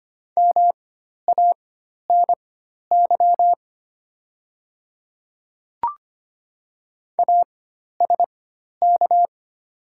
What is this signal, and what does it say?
Morse code, heard as a keyed tone, about 25 wpm, spaced as for 12 wpm, 700 Hz. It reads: MANY ASK
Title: 4x Character Spacing - Top 200 Words - 25wpm